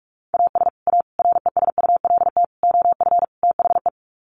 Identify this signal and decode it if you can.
Morse code, heard as a keyed tone, about 44 wpm, 700 Hz. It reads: AS A RESULT OF THE
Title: Sets of 5 Words 45wpm